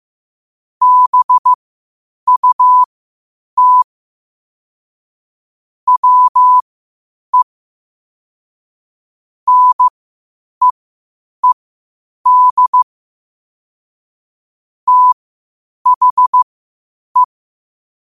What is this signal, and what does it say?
Morse code, heard as a keyed tone, about 15 wpm, spaced as for 8 wpm, 1000 Hz. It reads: BUT WE NEED THE